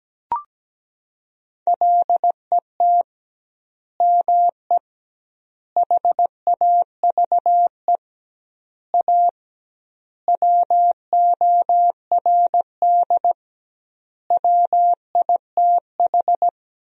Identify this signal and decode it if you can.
Morse code, heard as a keyed tone, about 17 wpm, 700 Hz. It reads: LET ME HAVE A WORD WITH
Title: 2x Word-Spacing - Sentences from Top 300 Words - 17wpm